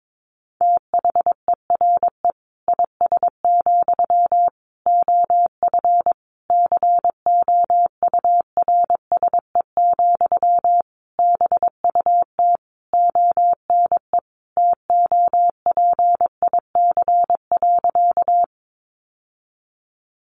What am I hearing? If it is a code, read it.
Morse code, heard as a keyed tone, about 22 wpm, 700 Hz. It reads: THERE IS, OF COURSE, BUT ONE TOPIC.
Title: Sentences from Top 1200 Words 22wpm